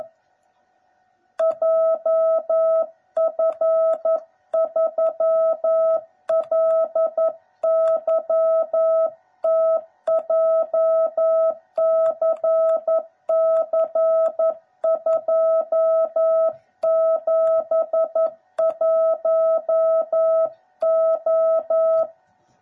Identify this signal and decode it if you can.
Morse code, heard as a keyed tone, about 11 wpm, 660 Hz. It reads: JF3LYT JCC271O
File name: jh8cbh / 2021cw / cw09b